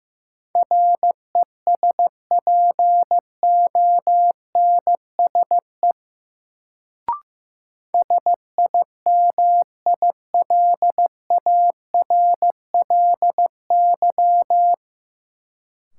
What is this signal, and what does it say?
Morse code, heard as a keyed tone, about 15 wpm, 700 Hz. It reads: RESPONSE SIMILARLY
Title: Sentences from Top 1400 Words - New Word Drill Alt - 15wpm